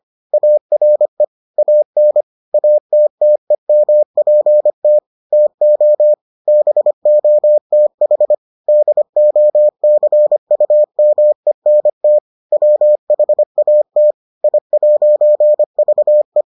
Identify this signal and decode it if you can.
Morse code, heard as a keyed tone, about 25 wpm, 600 Hz. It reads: ARE AN ATTEMPT TO BOTH DOCUMENT WHAT I'VE